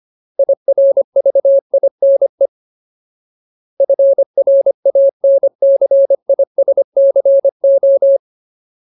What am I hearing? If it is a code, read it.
Morse code, heard as a keyed tone, about 25 wpm, 550 Hz. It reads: IRVINE FRANCISCO
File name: Ultimate Word List / 25WPM Ultimate